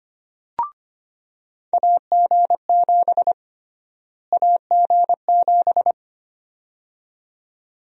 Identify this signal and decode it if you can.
Morse code, heard as a keyed tone, about 25 wpm, 700 Hz. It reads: AG7 AG7
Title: USA Call Sign Prefixes - Repeated 2x - 25wpm